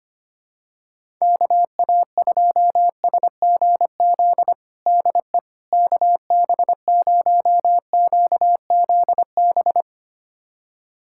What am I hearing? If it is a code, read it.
Morse code, heard as a keyed tone, about 25 wpm, 700 Hz. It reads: KA2SGZ DE KB0QZB
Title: USA Callsign DE Callsign 25wpm